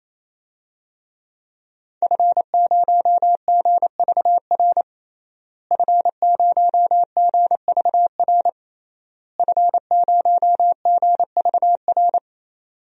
Morse code, heard as a keyed tone, about 28 wpm, 700 Hz. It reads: F0GVR F0GVR F0GVR
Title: DXCC-100 Call Signs - 5 Characters - Repeated 3x - 28wpm